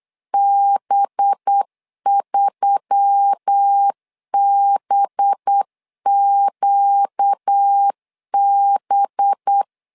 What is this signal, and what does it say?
Morse code, heard as a keyed tone, about 8 wpm, 800 Hz. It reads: B3BQB
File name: LEZIONI / 5 LBVC / 5 LB 8wpm 2816